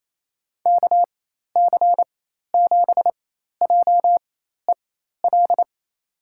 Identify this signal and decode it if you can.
Morse code, heard as a keyed tone, about 27 wpm, spaced as for 13 wpm, 700 Hz. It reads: KC7JEL
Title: USA Call Signs - 6 Characters - 4x Character Spacing - 28wpm